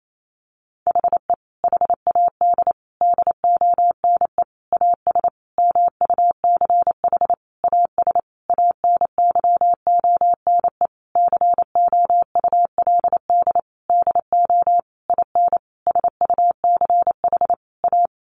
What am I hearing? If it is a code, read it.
Morse code, heard as a keyed tone, about 28 wpm, 700 Hz. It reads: HE HAD DONE AS MUCH AS ANYONE COULD DO IN SUCH A
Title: Sentences from Top 1200 Words 28wpm